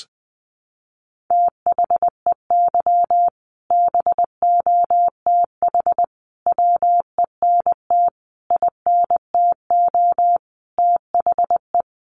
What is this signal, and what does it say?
Morse code, heard as a keyed tone, about 20 wpm, 700 Hz. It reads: THEY BOTH WENT INTO THE
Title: Sentences from Top 600 Words 20wpm